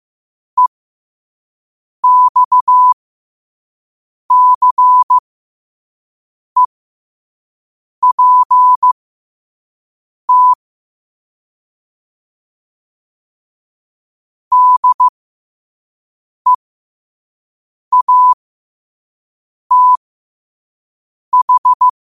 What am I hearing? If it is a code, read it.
Morse code, heard as a keyed tone, about 15 wpm, spaced as for 5 wpm, 1000 Hz. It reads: EXCEPT DEATH
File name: qotdfiles / 5wpm